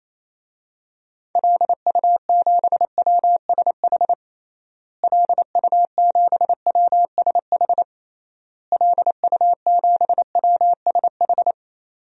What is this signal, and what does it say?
Morse code, heard as a keyed tone, about 28 wpm, 700 Hz. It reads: LU7WSH LU7WSH LU7WSH
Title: DXCC-100 Call Signs - 6 Characters - Repeated 3x - 28wpm